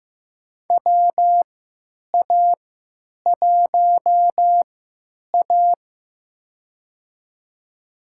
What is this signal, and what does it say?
Morse code, heard as a keyed tone, about 15 wpm, spaced as for 9 wpm, 700 Hz. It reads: WA1A
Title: USA Call Signs - 4 Characters - 3x Character Spacing - 15wpm